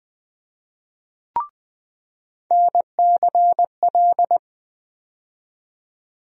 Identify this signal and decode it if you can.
Morse code, heard as a keyed tone, about 20 wpm, 700 Hz. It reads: NCL
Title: USA Call Sign Postfixes - 3 Letters 20wpm